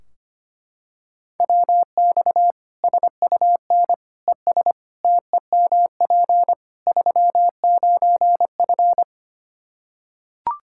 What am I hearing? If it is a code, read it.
Morse code, heard as a keyed tone, about 25 wpm, 700 Hz. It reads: WX SUN ES TEMP 39F
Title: Practice QSO - Exchange 2 - Info - 25wpm